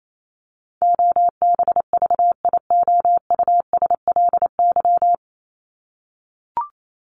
Morse code, heard as a keyed tone, about 28 wpm, 700 Hz. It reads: OBVIOUSLY
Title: Top 1501-1750 Words 28wpm